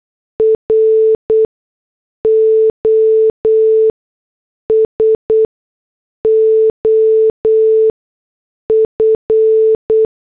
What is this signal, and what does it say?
Morse code, heard as a keyed tone, about 8 wpm, spaced as for 6 wpm, 440 Hz. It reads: ROS OF